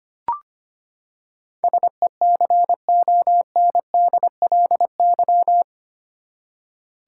Morse code, heard as a keyed tone, about 25 wpm, 700 Hz. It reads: SECONDLY
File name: Eight Letter Words - 25wpm